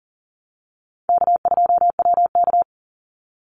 Morse code, heard as a keyed tone, about 40 wpm, 700 Hz. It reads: K2WK
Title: Easy Call Signs Repeated 3x - 40wpm